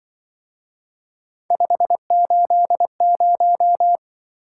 Morse code, heard as a keyed tone, about 23 wpm, 700 Hz. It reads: 580
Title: Speed-Racing - Three Letter-Number Combinations - 20wpm